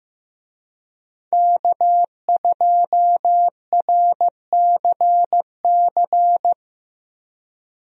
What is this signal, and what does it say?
Morse code, heard as a keyed tone, about 15 wpm, 700 Hz. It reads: K2RCC
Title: USA Call Signs - 5 Characters - Repeated 2x - 15wpm